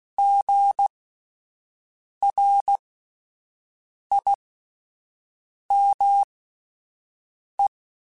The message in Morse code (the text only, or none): GRIME